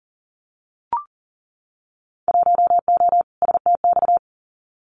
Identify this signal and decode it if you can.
Morse code, heard as a keyed tone, about 38 wpm, 700 Hz. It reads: 1O STX
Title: Winter Field Day Exchanges 40wpm